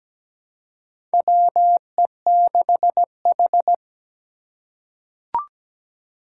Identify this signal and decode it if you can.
Morse code, heard as a keyed tone, about 17 wpm, 700 Hz. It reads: WE6H
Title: USA Call Signs - 4 Characters - 3x Character Spacing - 17wpm